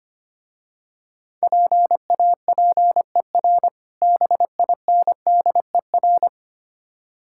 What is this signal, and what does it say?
Morse code, heard as a keyed tone, about 25 wpm, 700 Hz. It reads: PAPER BINDER
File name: practice / 25wpm / Difficult Words - Speed-Racing with Hints - 25wpm